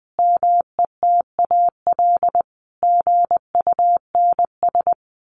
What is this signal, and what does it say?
Morse code, heard as a keyed tone, about 20 wpm, 700 Hz. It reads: METAL GUNS
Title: A Sound Of Thunder 20wpm